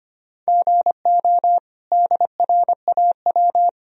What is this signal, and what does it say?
Morse code, heard as a keyed tone, about 25 wpm, 700 Hz. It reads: GO DRAW
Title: Sentences from Top 1000 Words - Review Part 1 - 25wpm